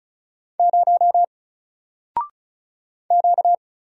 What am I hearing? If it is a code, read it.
Morse code, heard as a keyed tone, about 36 wpm, 700 Hz. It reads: 0 Q
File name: Single-Letter-Number-35wpm